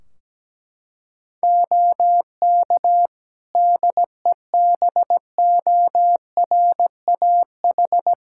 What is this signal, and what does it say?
Morse code, heard as a keyed tone, about 17 wpm, 700 Hz. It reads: OK DEBORAH
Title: Practice QSO - Exchange 2 - Info - 17wpm